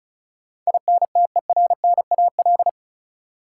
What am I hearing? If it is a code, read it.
Morse code, heard as a keyed tone, about 36 wpm, 700 Hz. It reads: INTERNAL
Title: Sentences from Top 1400 Words - New Word Drill - 35wpm